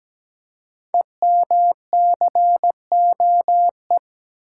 Morse code, heard as a keyed tone, about 17 wpm, 700 Hz. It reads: EMCOE
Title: Code Groups - 5 Letter:Numbers - 17wpm